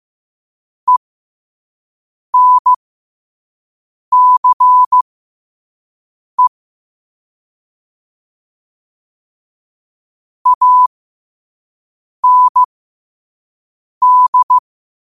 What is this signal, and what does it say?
Morse code, heard as a keyed tone, about 15 wpm, spaced as for 5 wpm, 1000 Hz. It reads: ENCE AND
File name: qotdfiles / 5wpm